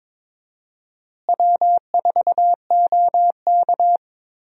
Morse code, heard as a keyed tone, about 21 wpm, 700 Hz. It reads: W4OK
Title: USA Call Signs - 4 Characters - Repeated 3x - 22wpm